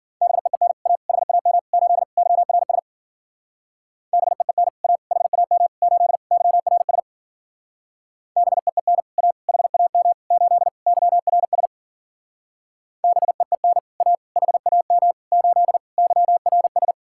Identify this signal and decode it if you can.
Morse code, heard as a keyed tone, about 51 wpm, 700 Hz. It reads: BEEN A HAM 8 YRS BEEN A HAM 8 YRS BEEN A HAM 8 YRS BEEN A HAM 8 YRS